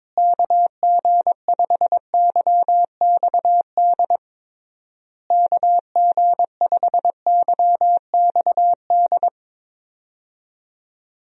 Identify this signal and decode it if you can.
Morse code, heard as a keyed tone, about 21 wpm, 700 Hz. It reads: KG5YXD KG5YXD
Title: USA Call Signs - 6 Characters - Repeated 3x - 22wpm